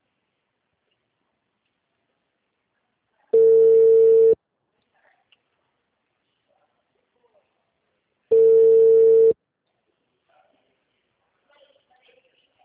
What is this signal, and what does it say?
Ringback tone (China, 450 Hz), with speech.
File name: out- -2011- - - .229